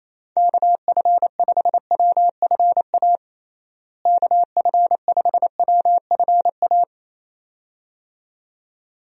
Morse code, heard as a keyed tone, about 28 wpm, 700 Hz. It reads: KF5WFA KF5WFA
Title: USA Call Signs - 6 Characters - Repeated 2x - 28wpm